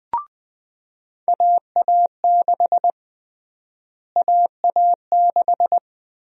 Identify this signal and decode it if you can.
Morse code, heard as a keyed tone, about 20 wpm, 700 Hz. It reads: AA6 AA6
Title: USA Call Sign Prefixes - Repeated 2x - 20wpm